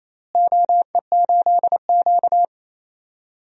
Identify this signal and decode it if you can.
Morse code, heard as a keyed tone, about 28 wpm, 700 Hz. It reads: OE8Q
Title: DXCC-100 Call Signs - 4 Characters - Repeated 2x - 28wpm